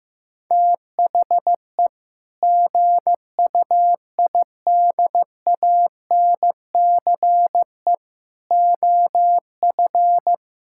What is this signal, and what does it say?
Morse code, heard as a keyed tone, about 16 wpm, 700 Hz. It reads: THE GUIDANCE OF
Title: Sentences from Top 1600 Words 15wpm